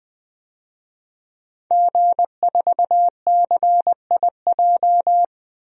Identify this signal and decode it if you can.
Morse code, heard as a keyed tone, about 21 wpm, 700 Hz. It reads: G4CIJ
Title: DXCC-100 Call Signs - 5 Characters - 20wpm